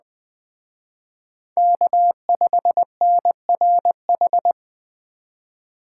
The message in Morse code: K5NRH

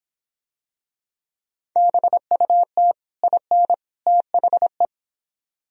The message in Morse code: BUT IN THE

